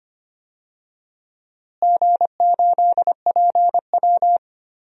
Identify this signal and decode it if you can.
Morse code, heard as a keyed tone, about 25 wpm, 700 Hz. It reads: G8PW
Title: DXCC-100 Call Signs - 4 Characters - Repeated 2x - 25wpm